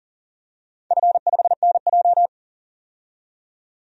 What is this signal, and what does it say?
Morse code, heard as a keyed tone, about 40 wpm, 700 Hz. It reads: F5NJ
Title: DXCC-100 Call Signs - 4 Characters - 40wpm